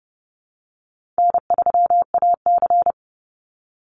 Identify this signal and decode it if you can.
Morse code, heard as a keyed tone, about 30 wpm, 700 Hz. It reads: N3AC
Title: Call Signs Repeated 2x - 30wpm